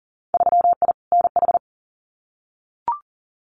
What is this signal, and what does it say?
Morse code, heard as a keyed tone, about 40 wpm, 700 Hz. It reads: 3I NH
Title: Winter Field Day Exchanges Encore 40wpm